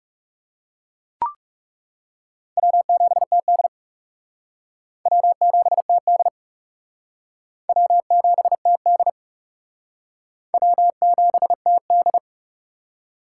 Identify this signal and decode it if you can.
Morse code, heard as a keyed tone, about 40 wpm, 700 Hz. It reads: W7TD W7TD W7TD W7TD